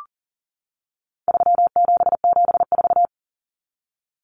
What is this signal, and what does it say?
Morse code, heard as a keyed tone, about 40 wpm, 700 Hz. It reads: 3774
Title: Common Cut Numbers - 4-Digit 40wpm